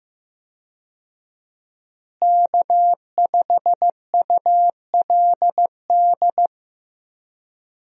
Morse code, heard as a keyed tone, about 15 wpm, 700 Hz. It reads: K5ULD
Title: USA Call Signs - 5 Characters - Repeated 2x - 15wpm